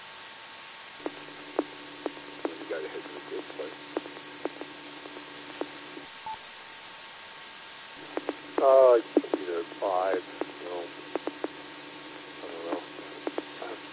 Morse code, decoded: none

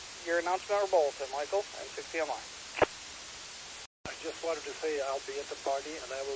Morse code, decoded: none

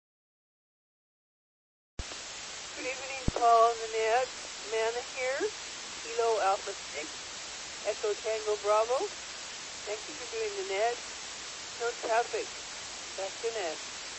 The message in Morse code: none

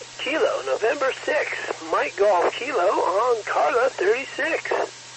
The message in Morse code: none

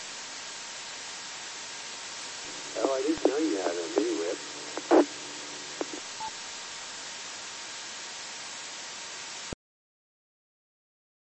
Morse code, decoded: none

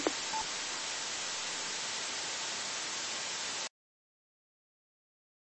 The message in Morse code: none

